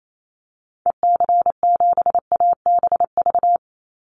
Morse code, heard as a keyed tone, about 28 wpm, 700 Hz. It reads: EC7ABV
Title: Call Signs - 28wpm